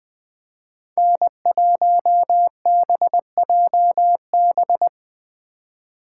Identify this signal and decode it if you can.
Morse code, heard as a keyed tone, about 21 wpm, 700 Hz. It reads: N1BJB
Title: Speed Racing - USA 1x3 Call Signs 17wpm